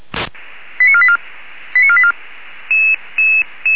Five-tone selective call (ZVEI), heard as quick runs of five tones, tones of about 70 ms. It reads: 89484 89484